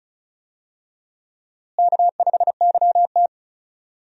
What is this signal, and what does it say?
Morse code, heard as a keyed tone, about 36 wpm, 700 Hz. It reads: K5YT
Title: USA Call Signs - 4 Characters - Repeated 2x - 35wpm